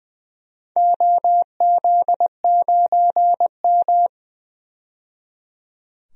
Morse code, heard as a keyed tone, about 20 wpm, 700 Hz. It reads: OZ9M
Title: DXCC-100 Call Signs - 4 Characters - 20wpm